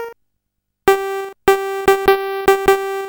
There is no code.